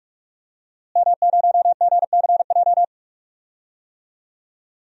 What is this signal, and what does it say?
Morse code, heard as a keyed tone, about 42 wpm, 700 Hz. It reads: M0GCJ
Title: DXCC-100 Call Signs - 5 Characters - Repeated 2x - 45wpm